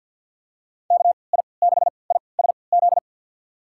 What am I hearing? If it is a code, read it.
Morse code, heard as a keyed tone, about 52 wpm, spaced as for 29 wpm, 700 Hz. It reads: KI6ISZ
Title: USA Call Signs - 3x Character Spacing - 50wpm